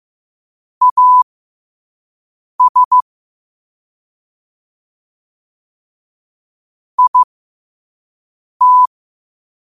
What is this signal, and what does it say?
Morse code, heard as a keyed tone, about 15 wpm, spaced as for 5 wpm, 1000 Hz. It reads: AS IT
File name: qotdfiles / 5wpm